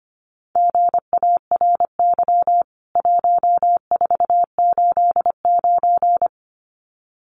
Morse code, heard as a keyed tone, about 25 wpm, 700 Hz. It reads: GARY 1489
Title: CWT Exchange 25wpm